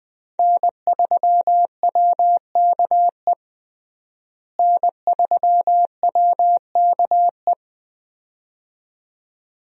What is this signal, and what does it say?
Morse code, heard as a keyed tone, about 20 wpm, 700 Hz. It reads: N3WKE N3WKE
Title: USA Call Signs - 5 Characters - Repeated 2x - 20wpm